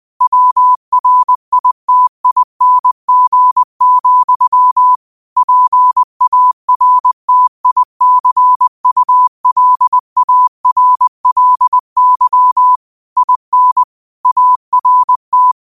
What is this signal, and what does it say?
Morse code, heard as a keyed tone, about 20 wpm, 1000 Hz. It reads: WRITING, PARTICULARLY IN ART